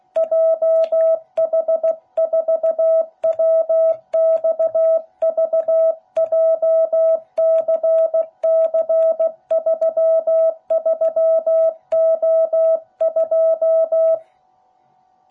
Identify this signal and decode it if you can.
Morse code, heard as a keyed tone, about 16 wpm, 660 Hz. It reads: JH4WXV JCC33O2